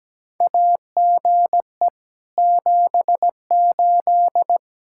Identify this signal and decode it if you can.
Morse code, heard as a keyed tone, about 17 wpm, 700 Hz. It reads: AGE 78
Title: Practice QSO - Exchange 3 - More Info - 17wpm